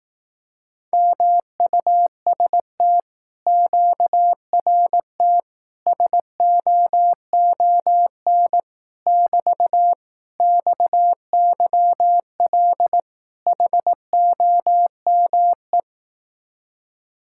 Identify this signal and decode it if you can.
Morse code, heard as a keyed tone, about 18 wpm, 700 Hz. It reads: MUST QRT SOON <BT> XYL HOME
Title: Speed-Racing Common QSO Phrases Encore - 15wpm